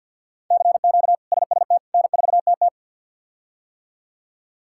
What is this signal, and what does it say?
Morse code, heard as a keyed tone, about 48 wpm, 700 Hz. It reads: CQ SST N4TT
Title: K1USN SST QSO Practice - 50wpm